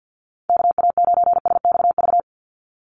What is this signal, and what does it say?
Morse code, heard as a keyed tone, about 50 wpm, 700 Hz. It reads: KA9SXV
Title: Call Signs - 50wpm